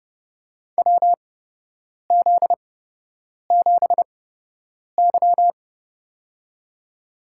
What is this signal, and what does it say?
Morse code, heard as a keyed tone, about 28 wpm, spaced as for 8 wpm, 700 Hz. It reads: WZ7Y
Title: USA Call Signs - 4 Characters - 8x Character Spacing - 30wpm